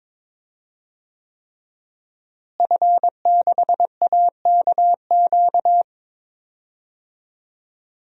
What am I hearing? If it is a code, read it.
Morse code, heard as a keyed tone, about 22 wpm, 700 Hz. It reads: F6AKQ
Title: DXCC-100 Call Signs - 5 Characters - Repeated 3x - 22wpm